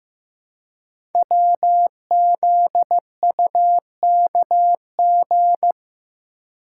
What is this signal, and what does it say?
Morse code, heard as a keyed tone, about 15 wpm, 700 Hz. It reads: WZUKG
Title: Code Groups - 5 Letters - 15wpm